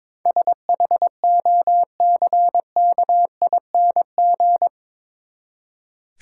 Morse code, heard as a keyed tone, about 22 wpm, 700 Hz. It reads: SHOCKING
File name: practice / 22wpm / Eight Letter Words - 22wpm